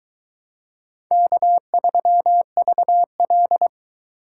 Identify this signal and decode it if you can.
Morse code, heard as a keyed tone, about 22 wpm, 700 Hz. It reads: K3VL